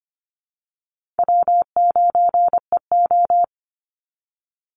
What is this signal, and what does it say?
Morse code, heard as a keyed tone, about 25 wpm, 700 Hz. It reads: W9EO
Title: Call Signs Repeated 2x - 25wpm